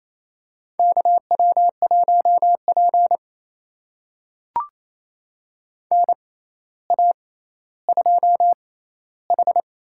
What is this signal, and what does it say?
Morse code, heard as a keyed tone, about 28 wpm, spaced as for 14 wpm, 700 Hz. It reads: KW1P NA2H
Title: USA Call Signs - 6x Character Spacing - 28wpm